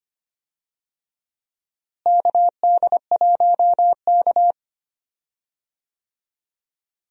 Morse code, heard as a keyed tone, about 25 wpm, 700 Hz. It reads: KD1K
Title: USA Call Signs - 4 Characters - Repeated 3x - 25wpm